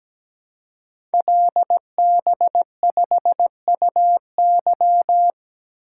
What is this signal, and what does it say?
Morse code, heard as a keyed tone, about 17 wpm, 690 Hz. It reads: LB5UY
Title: Code Groups - 5 Letter:Numbers - 17wpm